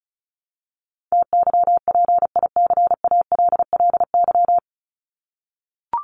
Morse code, heard as a keyed tone, about 34 wpm, 700 Hz. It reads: TYPICALLY
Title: LY Suffix - Top 100 Words with Base Words 35wpm